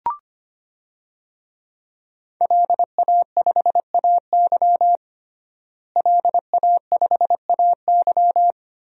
Morse code, heard as a keyed tone, about 25 wpm, 700 Hz. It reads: LA5AY LA5AY
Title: DXCC-100 Call Signs - 5 Characters - Repeated 2x - 25wpm